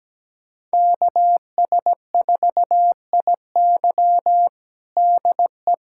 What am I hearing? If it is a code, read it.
Morse code, heard as a keyed tone, about 17 wpm, 700 Hz. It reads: KS4IY DE
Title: USA Callsign DE Callsign - 5 Characters - 17wpm